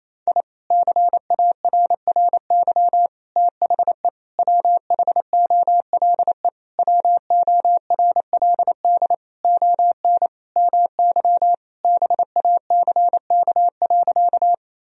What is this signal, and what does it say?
Morse code, heard as a keyed tone, about 28 wpm, 700 Hz. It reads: I CARRY THE WHOLE WORLD ON MY BACK.
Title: Sentences from Top 1000 Words - Review Part 1 - 28wpm